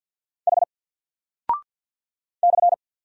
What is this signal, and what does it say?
Morse code, heard as a keyed tone, about 50 wpm, 700 Hz. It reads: H /
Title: Single Character (Letter:Number:Punctuation:Prosign) 50wpm